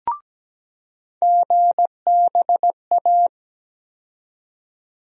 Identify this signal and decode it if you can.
Morse code, heard as a keyed tone, about 17 wpm, 700 Hz. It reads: GBA
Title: USA Call Sign Postfixes - 3 Letters 17wpm